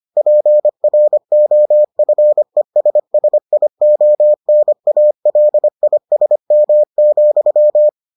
Morse code, heard as a keyed tone, about 25 wpm, 600 Hz. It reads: PROFESSIONALISM,